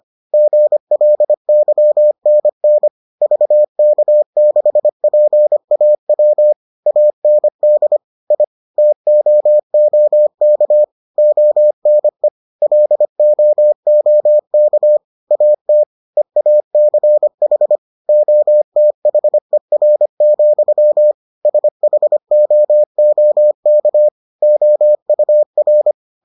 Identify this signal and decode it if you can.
Morse code, heard as a keyed tone, about 25 wpm, 600 Hz. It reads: GLYNN VK6PAW AND I TOOK ONE LOOK AT EACH OTHER, SHOOK OUR